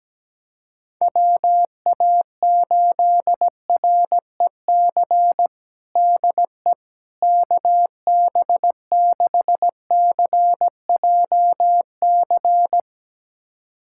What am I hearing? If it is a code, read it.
Morse code, heard as a keyed tone, about 17 wpm, 700 Hz. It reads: WA8REC DE KB6CJC
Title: USA Callsign DE Callsign - 6 Characters - 17wpm